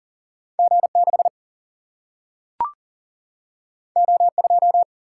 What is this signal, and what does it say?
Morse code, heard as a keyed tone, about 38 wpm, 700 Hz. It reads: G 6 O 2
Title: Single-Letter Single-Number 40wpm